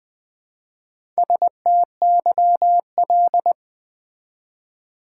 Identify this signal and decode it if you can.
Morse code, heard as a keyed tone, about 20 wpm, 700 Hz. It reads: STYL
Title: Build A Word - 1 Letter At A Time - 3 to 5-Letter Words - 20wpm